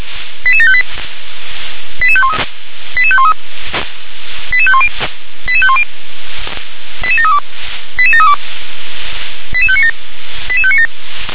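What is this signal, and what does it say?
Five-tone selective call (ZVEI), heard as quick runs of five tones, tones of about 70 ms. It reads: 88758 88513 88513 88510 88510 88732 88732 88597 88597
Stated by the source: ZVEI five-tone call received by SDR.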